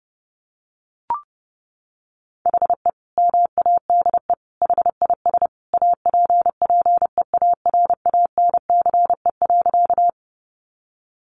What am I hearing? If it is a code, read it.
Morse code, heard as a keyed tone, about 30 wpm, 700 Hz. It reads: HE MADE HIS APPEARANCE.
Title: Sentences from Top 1100 Words 30wpm